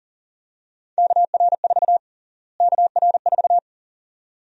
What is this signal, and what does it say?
Morse code, heard as a keyed tone, about 40 wpm, 700 Hz. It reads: KR4 KR4
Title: USA Call Sign Prefixes - Repeated 2x - 40wpm